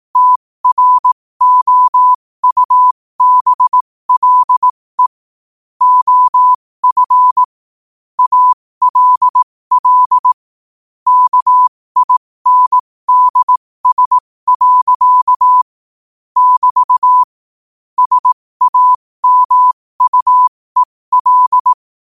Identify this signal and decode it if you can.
Morse code, heard as a keyed tone, about 18 wpm, spaced as for 15 wpm, 1000 Hz. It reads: TROUBLE OF ALL KINDS. <BT> SAMUEL
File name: qotdfiles / 15wpm